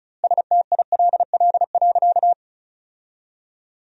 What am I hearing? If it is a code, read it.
Morse code, heard as a keyed tone, about 34 wpm, 700 Hz. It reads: STILL.
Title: 2x Word-Spacing - Sentences from Top 500 Words - 35wpm